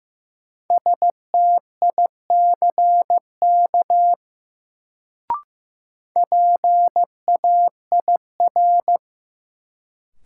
Morse code, heard as a keyed tone, about 15 wpm, 700 Hz. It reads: STICK PAIR
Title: Sentences from Top 1200 Words - New Word Drill Alt - 15wpm